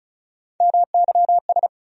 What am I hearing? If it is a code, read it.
Morse code, heard as a keyed tone, about 36 wpm, 700 Hz. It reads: MYS
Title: USA Call Sign Postfixes - 3 Letters 35wpm